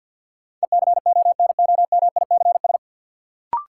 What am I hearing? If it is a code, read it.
Morse code, heard as a keyed tone, about 53 wpm, 700 Hz. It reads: ECONOMICS
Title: Sentences from Top 1000 Words - Review Part 3 - New Word Drill - 50wpm